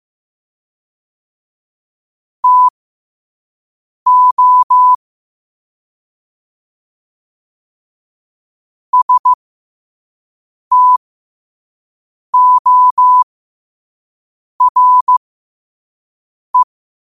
Morse code, heard as a keyed tone, about 15 wpm, spaced as for 5 wpm, 1000 Hz. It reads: TO STORE
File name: qotdfiles / 5wpm